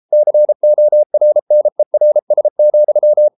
Morse code, heard as a keyed tone, about 33 wpm, 600 Hz. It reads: CORNERS,